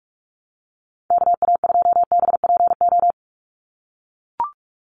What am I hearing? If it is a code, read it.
Morse code, heard as a keyed tone, about 44 wpm, 700 Hz. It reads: KA2BPO
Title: USA Call Signs 45wpm